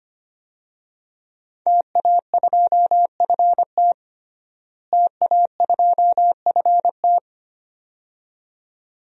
Morse code, heard as a keyed tone, about 25 wpm, 700 Hz. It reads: TA2FT TA2FT
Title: DXCC-100 Call Signs - 5 Characters - Repeated 2x - 25wpm